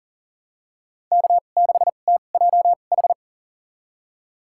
Morse code, heard as a keyed tone, about 40 wpm, spaced as for 29 wpm, 700 Hz. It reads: K6TJH
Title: USA Call Signs - 5 Characters - 2x Character Spacing - 40wpm